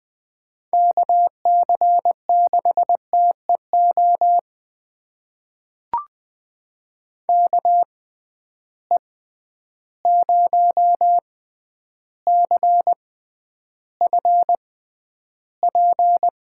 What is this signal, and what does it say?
Morse code, heard as a keyed tone, about 20 wpm, spaced as for 10 wpm, 700 Hz. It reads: KC6TEO KE0CFP
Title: USA Call Signs - 6 Characters - 6x Character Spacing - 20wpm